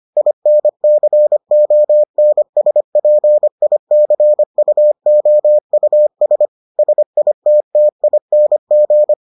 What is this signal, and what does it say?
Morse code, heard as a keyed tone, about 25 wpm, 600 Hz. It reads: INCONSPICUOUS SITTING